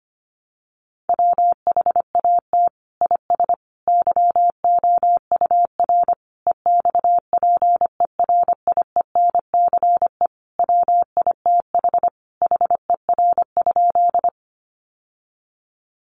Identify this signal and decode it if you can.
Morse code, heard as a keyed tone, about 25 wpm, 700 Hz. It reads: WHAT IS YOUR EXPERIENCE WITH HER?
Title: Sentences from Top 600 Words 25wpm